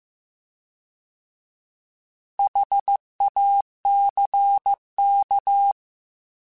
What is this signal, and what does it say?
Morse code, heard as a keyed tone, about 15 wpm, 800 Hz. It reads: HACK
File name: Alphabetical / Alphabetical mp3s / Hs @13